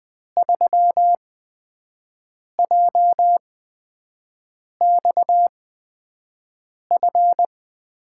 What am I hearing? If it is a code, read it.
Morse code, heard as a keyed tone, about 20 wpm, spaced as for 5 wpm, 700 Hz. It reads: 3JXF
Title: USA Call Signs - 6 Characters - 8x Character Spacing - 20wpm